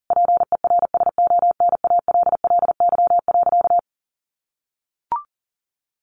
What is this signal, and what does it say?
Morse code, heard as a keyed tone, about 40 wpm, 700 Hz. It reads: PERSONALLY.
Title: Sentences from Top 1400 Words 40wpm